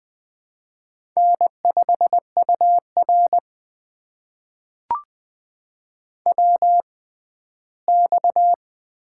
Morse code, heard as a keyed tone, about 20 wpm, spaced as for 14 wpm, 700 Hz. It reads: N5UR WX
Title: USA Call Signs - 4 Characters - 6x Character Spacing - 20wpm